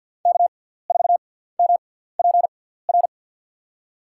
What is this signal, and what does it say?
Morse code, heard as a keyed tone, about 49 wpm, spaced as for 17 wpm, 700 Hz. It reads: K4MPR
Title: USA Call Signs - 6x Character Spacing - 50wpm